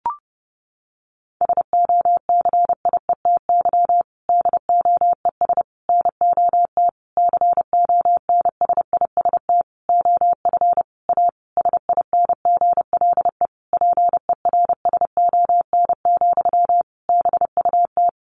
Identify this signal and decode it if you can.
Morse code, heard as a keyed tone, about 30 wpm, 700 Hz. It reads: SOCIETY DOES NOT CONSIST OF A SINGLE PERSON, BUT
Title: Sentences from Top 1400 Words 30wpm